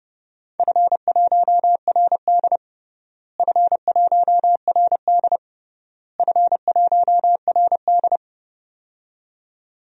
Morse code, heard as a keyed tone, about 30 wpm, 700 Hz. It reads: F1RD F1RD F1RD
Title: DXCC-100 Call Signs - 4 Characters - Repeated 3x - 30wpm